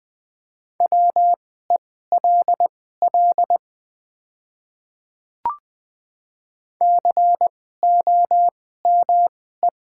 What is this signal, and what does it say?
Morse code, heard as a keyed tone, about 20 wpm, spaced as for 15 wpm, 700 Hz. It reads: WELL COME
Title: 2x Character Spacing - Top 300 Words - 20wpm